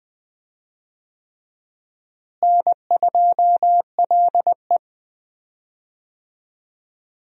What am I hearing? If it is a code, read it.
Morse code, heard as a keyed tone, about 20 wpm, 700 Hz. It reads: N2LE
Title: USA Call Signs - 4 Characters - Repeated 3x - 20wpm